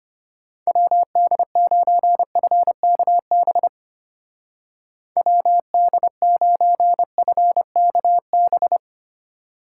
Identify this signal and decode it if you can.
Morse code, heard as a keyed tone, about 33 wpm, 700 Hz. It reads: WD9FKB WD9FKB